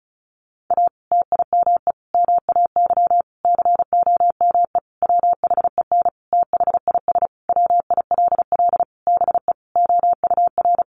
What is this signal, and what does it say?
Morse code, heard as a keyed tone, about 34 wpm, 700 Hz. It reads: A TIME MAY COME WHEN THIS WILL BE OUR
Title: Sentences from Top 1500 Words 35wpm